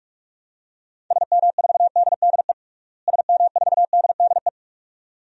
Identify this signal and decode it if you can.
Morse code, heard as a keyed tone, about 46 wpm, 700 Hz. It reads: SM4DDE SM4DDE
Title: DXCC-100 Call Signs - 6 Characters - Repeated 2x - 45wpm